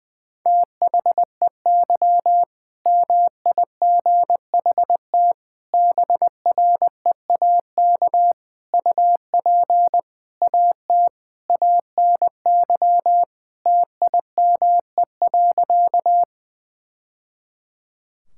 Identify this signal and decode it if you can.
Morse code, heard as a keyed tone, about 21 wpm, 700 Hz. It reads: THEY MIGHT BREAK UP AT ANY TIME.
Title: Sentences from Top 1000 Words - Review Part 6 - 20wpm